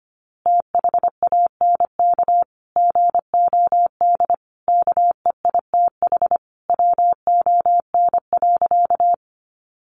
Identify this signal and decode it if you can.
Morse code, heard as a keyed tone, about 25 wpm, 700 Hz. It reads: THANK GOD KEITH WON.
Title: A Sound Of Thunder 25wpm